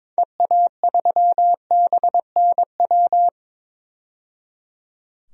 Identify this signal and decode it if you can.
Morse code, heard as a keyed tone, about 21 wpm, 700 Hz. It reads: EA3BNW
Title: DXCC-100 Call Signs - 6 Characters - 22wpm